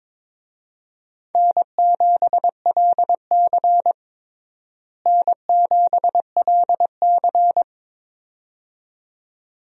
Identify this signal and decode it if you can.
Morse code, heard as a keyed tone, about 22 wpm, 700 Hz. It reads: N7LC N7LC
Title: USA Call Signs - 4 Characters - Repeated 2x - 22wpm